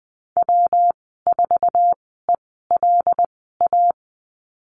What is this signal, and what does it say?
Morse code, heard as a keyed tone, about 20 wpm, spaced as for 15 wpm, 700 Hz. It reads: W4ELA
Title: 167 Lesson 27g - Number 7 - Call Signs 20wpm